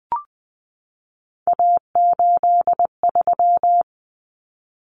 Morse code, heard as a keyed tone, about 20 wpm, 700 Hz. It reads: A83
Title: Common Cut Numbers - 3-Digit 20wpm